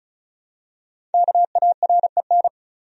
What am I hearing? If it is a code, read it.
Morse code, heard as a keyed tone, about 34 wpm, 700 Hz. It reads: KAREN
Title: USA Common Names - 35wpm